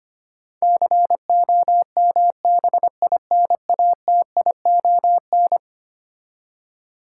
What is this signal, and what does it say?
Morse code, heard as a keyed tone, about 25 wpm, 700 Hz. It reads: COMBINATION
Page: morse-code-ninja-practice-25wpm